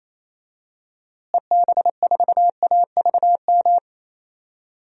Morse code, heard as a keyed tone, about 28 wpm, 700 Hz. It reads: EB4AVM
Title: DXCC-100 Call Signs - 6 Characters - Repeated 2x - 28wpm